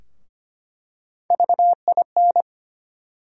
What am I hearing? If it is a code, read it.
Morse code, heard as a keyed tone, about 25 wpm, 700 Hz. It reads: VIN
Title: USA Call Sign Postfixes - 3 Letters 25wpm